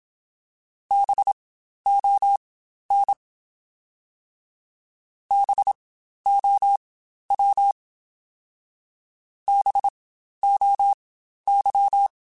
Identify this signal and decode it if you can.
Morse code, heard as a keyed tone, about 27 wpm, spaced as for 12 wpm, 800 Hz. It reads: BON BOW BOY
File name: Spaced Alphas / Spaced Alphas mp3s / Bs sp @25